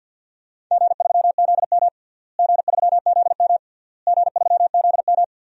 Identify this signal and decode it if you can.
Morse code, heard as a keyed tone, about 51 wpm, 700 Hz. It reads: G3ZM G3ZM G3ZM
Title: DXCC-100 Call Signs - 4 Characters - Repeated 3x - 50wpm